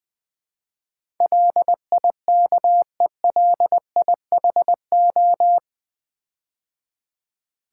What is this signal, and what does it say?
Morse code, heard as a keyed tone, about 20 wpm, 700 Hz. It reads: LIKELIHO
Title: Build A Word - 4 Letters At A Time - 10-Letter Words - 20wpm